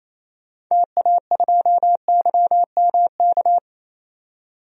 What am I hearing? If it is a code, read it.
Morse code, heard as a keyed tone, about 28 wpm, 700 Hz. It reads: TA2YMK
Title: DXCC-100 Call Signs - 6 Characters - 28wpm